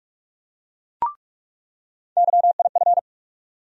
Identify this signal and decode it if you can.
Morse code, heard as a keyed tone, about 46 wpm, 700 Hz. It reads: YIF
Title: USA Call Sign Postfixes - 3 Letters 45wpm